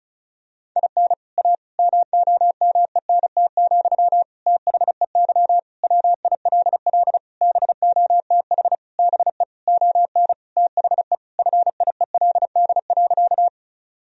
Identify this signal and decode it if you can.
Morse code, heard as a keyed tone, about 34 wpm, 700 Hz. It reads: IN A MOMENT, THEY WILL BOTH BE ON THE FIELD.
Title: Sentences from Top 1000 Words - Review Part 2 - 35wpm